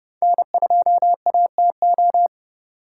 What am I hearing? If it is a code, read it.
Morse code, heard as a keyed tone, about 34 wpm, 700 Hz. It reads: N2ATO